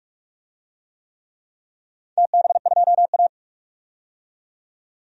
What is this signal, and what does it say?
Morse code, heard as a keyed tone, about 44 wpm, 700 Hz. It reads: TB2A
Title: DXCC-100 Call Signs - 4 Characters - 45wpm